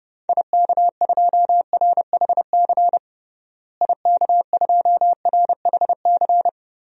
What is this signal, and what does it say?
Morse code, heard as a keyed tone, about 32 wpm, 700 Hz. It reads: IK2RHC IK2RHC
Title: DXCC-100 Call Signs - 6 Characters - Repeated 2x - 30wpm